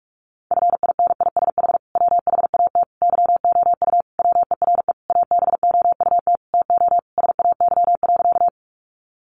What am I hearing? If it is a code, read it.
Morse code, heard as a keyed tone, about 44 wpm, 700 Hz. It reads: FINISH WHAT YOU WERE ABOUT TO SAY.
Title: Sentences from Top 1200 Words 45wpm